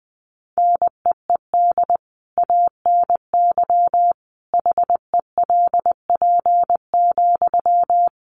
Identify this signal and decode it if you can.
Morse code, heard as a keyed tone, about 20 wpm, 700 Hz. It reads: NEED ANY HELP,
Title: Sentences from Top 1200 Words 20wpm